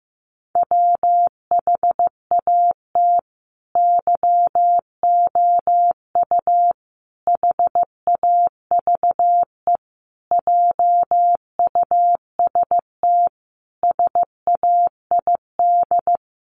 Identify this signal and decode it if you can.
Morse code, heard as a keyed tone, about 15 wpm, 700 Hz. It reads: WHAT YOU HAVE JUST SAID
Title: Sentences from Top 200 Words 15wpm